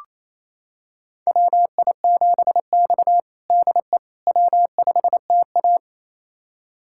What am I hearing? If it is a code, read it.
Morse code, heard as a keyed tone, about 28 wpm, 700 Hz. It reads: WI7X DE W5TA
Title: USA Callsign DE Callsign - 4 Characters - 28wpm